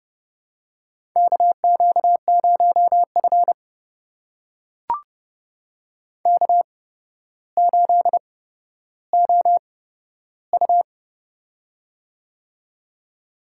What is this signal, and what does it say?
Morse code, heard as a keyed tone, about 30 wpm, spaced as for 13 wpm, 700 Hz. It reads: KQ0F K8OU
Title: USA Call Signs - 4 Characters - 8x Character Spacing - 30wpm